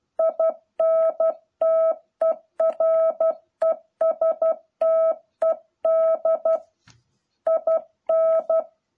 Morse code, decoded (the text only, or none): INTERESTED IN